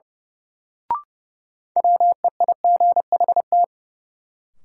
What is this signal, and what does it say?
Morse code, heard as a keyed tone, about 30 wpm, 700 Hz. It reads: WEIGHT
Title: Sentences from Top 800 Words - New Word Drill - 30wpm